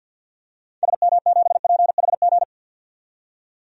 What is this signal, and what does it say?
Morse code, heard as a keyed tone, about 46 wpm, 700 Hz. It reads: SM7PHG
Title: DXCC-100 Call Signs - 6 Characters - 50wpm